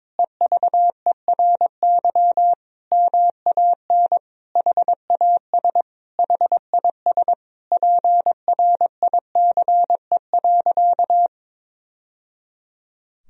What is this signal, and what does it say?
Morse code, heard as a keyed tone, about 22 wpm, 700 Hz. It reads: EVERY MAN HAS HIS PRICE.
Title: Sentences from Top 1000 Words - Review Part 4 - 22wpm